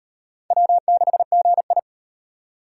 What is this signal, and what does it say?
Morse code, heard as a keyed tone, about 36 wpm, 700 Hz. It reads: W6GI